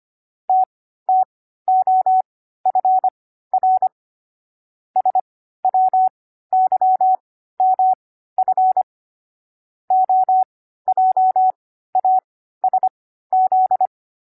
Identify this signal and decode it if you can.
Morse code, heard as a keyed tone, about 25 wpm, spaced as for 14 wpm, 750 Hz. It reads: TTOFR SWYMF OJASZ
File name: CW-Koch / koch.29.3.groups5.15wpm